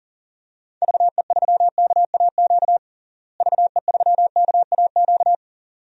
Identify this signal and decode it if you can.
Morse code, heard as a keyed tone, about 40 wpm, 700 Hz. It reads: VE3KAQ VE3KAQ
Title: DXCC-100 Call Signs - 6 Characters - Repeated 2x - 40wpm